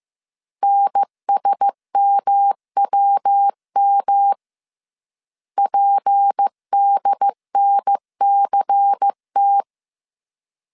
Morse code, heard as a keyed tone, about 15 wpm, 800 Hz. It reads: NSMWM PDNCT